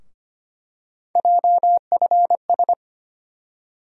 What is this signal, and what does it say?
Morse code, heard as a keyed tone, about 25 wpm, 700 Hz. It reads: JFS
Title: USA Call Sign Postfixes - 3 Letters 25wpm